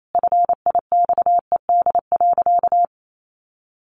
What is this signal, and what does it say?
Morse code, heard as a keyed tone, about 28 wpm, 700 Hz. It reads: FIXED.
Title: Sentences from Top 1100 Words 28wpm